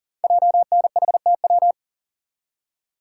Morse code, heard as a keyed tone, about 42 wpm, 690 Hz. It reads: JNHTW
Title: Code Groups - 5 Letters - 40wpm